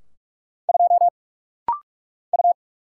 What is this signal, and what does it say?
Morse code, heard as a keyed tone, about 46 wpm, 700 Hz. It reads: 2 U
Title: Single Letter-Number - Rapid-Fire 45wpm